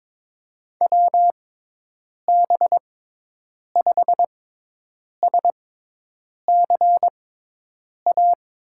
Morse code, heard as a keyed tone, about 22 wpm, spaced as for 8 wpm, 700 Hz. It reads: WB5SCA
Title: USA Call Signs - 6 Characters - 6x Character Spacing - 22wpm